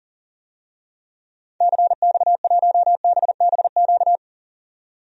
Morse code, heard as a keyed tone, about 40 wpm, 700 Hz. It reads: CX1BBQ
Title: DXCC-100 Call Signs - 6 Characters - 40wpm